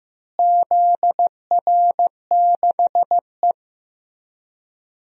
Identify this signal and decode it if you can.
Morse code, heard as a keyed tone, about 15 wpm, 700 Hz. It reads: ZR6E
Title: DXCC-100 Call Signs - 4 Characters - Repeated 3x - 15wpm